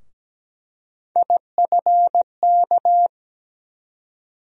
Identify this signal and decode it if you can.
Morse code, heard as a keyed tone, about 17 wpm, 700 Hz. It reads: IFK
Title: USA Call Sign Postfixes - 3 Letters 17wpm